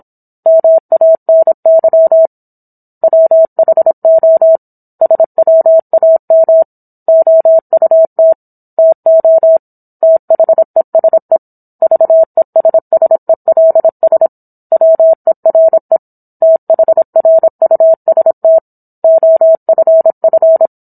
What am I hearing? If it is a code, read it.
Morse code, heard as a keyed tone, about 27 wpm, 650 Hz. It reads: MANY WHO SWAM OUT TO THESE VESSELS WERE THRUST OFF